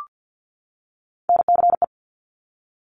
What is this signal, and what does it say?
Morse code, heard as a keyed tone, about 50 wpm, 700 Hz. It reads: NCE
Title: Top Three Letter Combinations in Words 50wpm